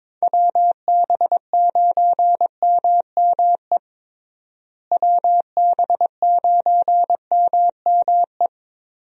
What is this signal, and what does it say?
Morse code, heard as a keyed tone, about 21 wpm, 700 Hz. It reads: WB9MME WB9MME
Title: USA Call Signs - 6 Characters - Repeated 3x - 22wpm